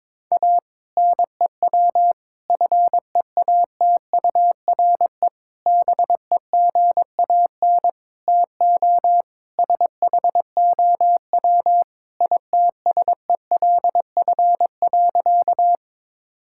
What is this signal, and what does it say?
Morse code, heard as a keyed tone, about 21 wpm, 700 Hz. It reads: A NEW FEATURE BEGAN TO SHOW ITSELF.
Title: Sentences from Top 1000 Words - Review Part 5 - 22wpm